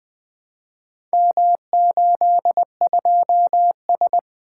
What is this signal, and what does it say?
Morse code, heard as a keyed tone, about 20 wpm, 700 Hz. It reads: M82S
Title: Four Letter-Number Combinations - 20wpm